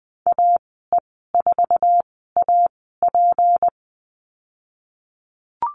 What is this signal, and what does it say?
Morse code, heard as a keyed tone, about 20 wpm, spaced as for 15 wpm, 700 Hz. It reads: AE4AP
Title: 173 Lesson 28g - Number 9 - Call Signs 20wpm